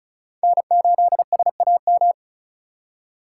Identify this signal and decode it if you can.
Morse code, heard as a keyed tone, about 34 wpm, 700 Hz. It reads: N8SAM
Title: USA Call Signs - 5 Characters - 2x Character Spacing - 35wpm